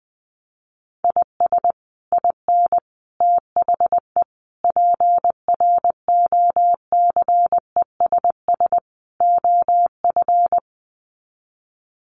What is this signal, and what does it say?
Morse code, heard as a keyed tone, about 20 wpm, 700 Hz. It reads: IS IN THE PROCESS OF
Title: Sets of 5 Words Encore2 20wpm